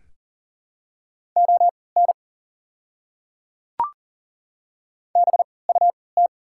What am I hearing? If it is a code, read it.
Morse code, heard as a keyed tone, about 40 wpm, spaced as for 23 wpm, 700 Hz. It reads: ON BUT
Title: 3x Character Spacing - Top 300 Words - 40wpm